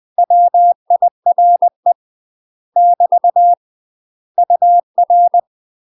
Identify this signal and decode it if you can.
Morse code, heard as a keyed tone, about 20 wpm, 700 Hz. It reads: WIRE <BT> UR
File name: UK MASTER QSOs TRAINING / 20WPM / UK MASTER QSOs 0066